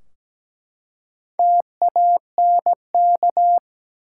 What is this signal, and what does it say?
Morse code, heard as a keyed tone, about 17 wpm, 700 Hz. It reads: TANK
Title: Four Letter Words - 17wpm